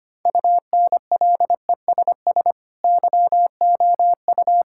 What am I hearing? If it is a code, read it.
Morse code, heard as a keyed tone, about 25 wpm, 700 Hz. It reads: UNLESS YOU
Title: Sentences from Top 1600 Words 25wpm